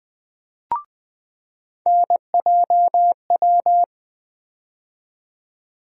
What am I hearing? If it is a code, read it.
Morse code, heard as a keyed tone, about 20 wpm, 700 Hz. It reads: NJW
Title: USA Call Sign Postfixes - 3 Letters 20wpm